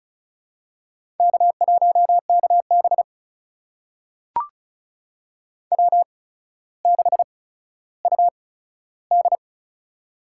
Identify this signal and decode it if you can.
Morse code, heard as a keyed tone, about 36 wpm, spaced as for 15 wpm, 700 Hz. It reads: K1KB W6UD
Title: USA Call Signs - 4 Characters - 8x Character Spacing - 35wpm